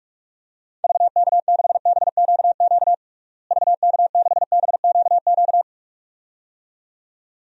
Morse code, heard as a keyed tone, about 46 wpm, 700 Hz. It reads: VK6BQQ VK6BQQ
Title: DXCC-100 Call Signs - 6 Characters - Repeated 2x - 45wpm